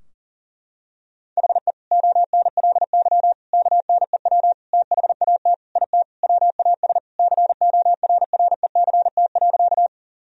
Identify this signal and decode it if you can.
Morse code, heard as a keyed tone, about 40 wpm, 700 Hz. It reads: HE ONLY KNEW THAT IT WAS CORRECT.
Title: Sentences from Top 1000 Words - Review Part 2 - 40wpm